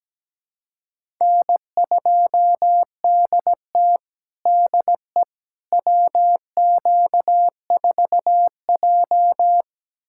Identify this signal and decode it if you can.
Morse code, heard as a keyed tone, about 17 wpm, 700 Hz. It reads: N2DT DE WQ4J
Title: USA Callsign DE Callsign - 4 Characters - 17wpm